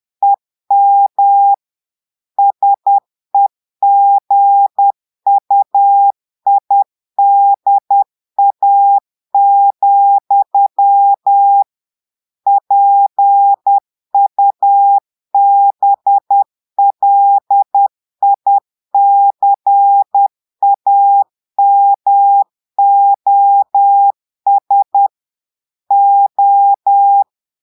Morse code, heard as a keyed tone, about 10 wpm, 800 Hz. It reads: EM SEGUIDA, PUBLICAMOS O
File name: Portugais - texte en clair 1 - vitesse - 10 mots minute